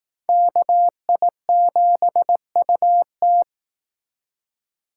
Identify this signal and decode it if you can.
Morse code, heard as a keyed tone, about 19 wpm, 700 Hz. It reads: KI7UT